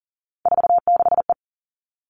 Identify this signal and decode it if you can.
Morse code, heard as a keyed tone, about 40 wpm, 700 Hz. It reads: 46E
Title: All Cut Numbers - 3-Digit 40wpm